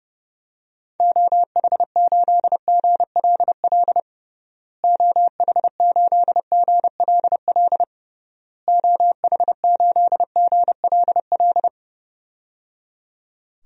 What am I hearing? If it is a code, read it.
Morse code, heard as a keyed tone, about 30 wpm, 700 Hz. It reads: OH8GLL OH8GLL OH8GLL
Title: DXCC-100 Call Signs - 6 Characters - Repeated 3x - 30wpm